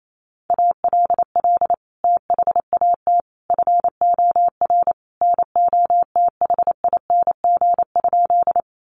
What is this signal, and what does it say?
Morse code, heard as a keyed tone, about 28 wpm, 700 Hz. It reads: ALL THAT FOR NOTHING?
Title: Sentences from Top 500 Words 28wpm